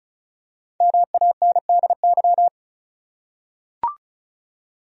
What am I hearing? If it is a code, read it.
Morse code, heard as a keyed tone, about 34 wpm, 700 Hz. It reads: MANDY
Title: USA Common Names - 35wpm